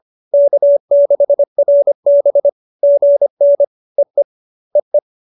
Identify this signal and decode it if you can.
Morse code, heard as a keyed tone, about 25 wpm, 570 Hz. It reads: K6RB GN EE EE
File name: QSO301 25